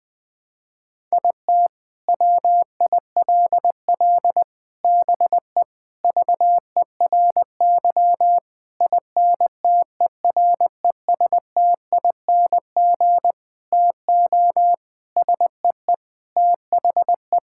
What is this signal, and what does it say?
Morse code, heard as a keyed tone, about 20 wpm, 700 Hz. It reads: IT WILL BE VERY INTERESTING TO SEE THE
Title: Sentences with 7-Word Phrases - Spoken Beginning - 20wpm